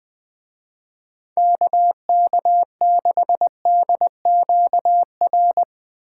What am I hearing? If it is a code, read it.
Morse code, heard as a keyed tone, about 20 wpm, 700 Hz. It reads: KK6DQR
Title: Speed Racing - USA 2x3 Call Signs 17wpm